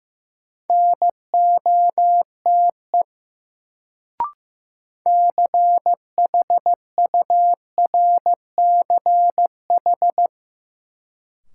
Morse code, heard as a keyed tone, about 15 wpm, 700 Hz. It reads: NOTE CHURCH
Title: Sets of 5 Words - New Word Drill - 15wpm